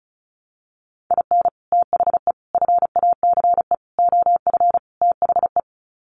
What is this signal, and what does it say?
Morse code, heard as a keyed tone, about 34 wpm, 700 Hz. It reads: IN THE FACE OF THE
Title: Sets of 5 Words 35wpm